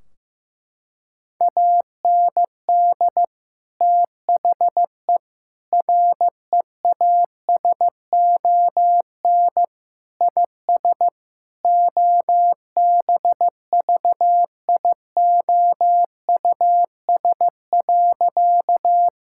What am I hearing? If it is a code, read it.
Morse code, heard as a keyed tone, about 15 wpm, 700 Hz. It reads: AND THE REASON IS OBVIOUS.
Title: Sentences from Top 1600 Words 15wpm